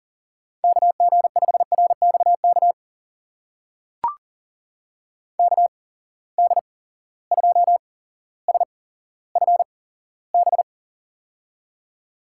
Morse code, heard as a keyed tone, about 40 wpm, spaced as for 17 wpm, 700 Hz. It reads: KG5RXK KD2SFB
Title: USA Call Signs - 6 Characters - 8x Character Spacing - 40wpm